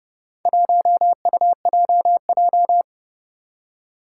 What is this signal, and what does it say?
Morse code, heard as a keyed tone, about 30 wpm, 700 Hz. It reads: 1UJJ
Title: Four Challenging Letter-Number Combinations 30wpm